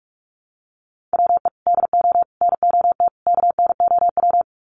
Morse code, heard as a keyed tone, about 44 wpm, 700 Hz. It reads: WE DO NOT KNOW
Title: Sets of 4 Words 45wpm